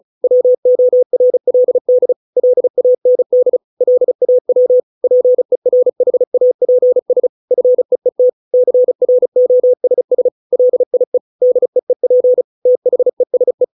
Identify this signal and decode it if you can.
Morse code, heard as a keyed tone, about 34 wpm, 500 Hz. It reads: WORLD LAND LAW PERHAPS FEET CROSS LIE DEEP THESE